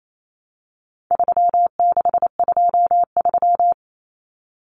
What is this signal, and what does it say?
Morse code, heard as a keyed tone, about 28 wpm, 700 Hz. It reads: 3623
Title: Numbers - 4-Digit 28wpm